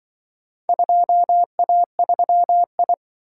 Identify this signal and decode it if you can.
Morse code, heard as a keyed tone, about 25 wpm, 700 Hz. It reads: 2A3I